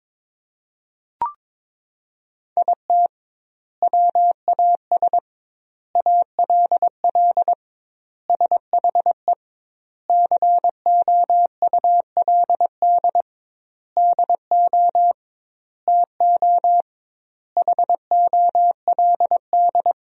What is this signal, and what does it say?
Morse code, heard as a keyed tone, about 22 wpm, 700 Hz. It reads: IT WAS ALL SHE COULD DO TO HOLD
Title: 2x Word-Spacing - Sentences from Top 400 Words - 22wpm